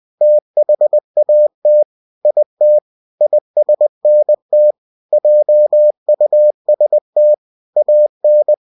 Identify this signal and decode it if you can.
Morse code, heard as a keyed tone, about 20 wpm, 600 Hz. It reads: THAT IT ISNT JUST AN
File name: cw-books / winnie-the-pooh / winnie-20wpm-page-0099